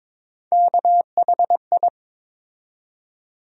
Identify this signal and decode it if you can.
Morse code, heard as a keyed tone, about 22 wpm, 700 Hz. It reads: KHI
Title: USA Call Sign Postfixes - 3 Letters 22wpm